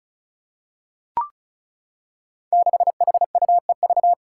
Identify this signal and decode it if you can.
Morse code, heard as a keyed tone, about 34 wpm, 700 Hz. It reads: 6HUEV
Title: Five Challenging Letter-Number Combinations 35wpm